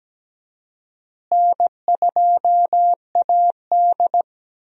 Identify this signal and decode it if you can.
Morse code, heard as a keyed tone, about 17 wpm, 700 Hz. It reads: N2AD
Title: USA Call Signs - 4 Characters - Repeated 3x - 17wpm